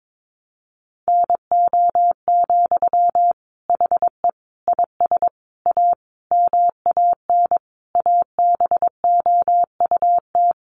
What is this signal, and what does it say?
Morse code, heard as a keyed tone, about 22 wpm, 700 Hz. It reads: NO, HE IS A MAN ABOUT
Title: Sentences from Top 400 Words 22wpm